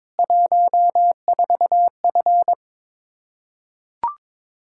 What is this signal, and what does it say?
Morse code, heard as a keyed tone, about 22 wpm, 700 Hz. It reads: 14F
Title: K1USN SST QSO Practice 22wpm